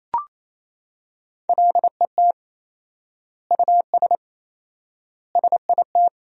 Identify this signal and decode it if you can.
Morse code, heard as a keyed tone, about 28 wpm, 700 Hz. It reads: LET US SIT
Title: 4x Word-Spacing - Sentences from Top 500 Words - 28wpm